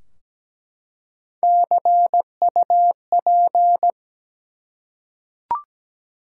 Morse code, heard as a keyed tone, about 17 wpm, 700 Hz. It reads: CUP
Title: Three Letter Words - 17wpm